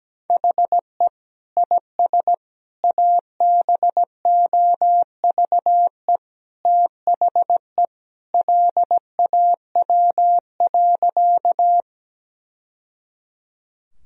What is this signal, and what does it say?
Morse code, heard as a keyed tone, about 17 wpm, 700 Hz. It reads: HE IS ABOVE THE LAW.
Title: Sentences from Top 1000 Words - Review Part 3 - 17wpm